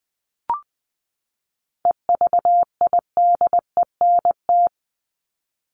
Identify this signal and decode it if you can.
Morse code, heard as a keyed tone, about 20 wpm, 700 Hz. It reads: EVIDENT
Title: LY Suffix - Top 100 Words with Base Words 20wpm